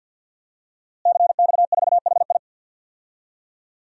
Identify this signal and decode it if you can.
Morse code, heard as a keyed tone, about 52 wpm, 700 Hz. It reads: CX4HI
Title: DXCC-100 Call Signs - 5 Characters - Repeated 2x - 50wpm